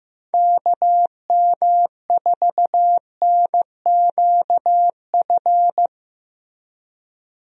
Morse code, heard as a keyed tone, about 15 wpm, 700 Hz. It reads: KM4NQF
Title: USA Call Signs - 6 Characters - Repeated 3x - 15wpm